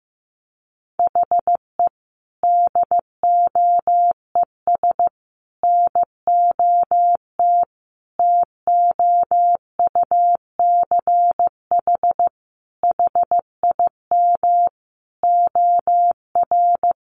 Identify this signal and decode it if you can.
Morse code, heard as a keyed tone, about 15 wpm, 700 Hz. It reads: HE DOES NOT TOUCH HIM OR
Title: Sentences from Top 1400 Words 15wpm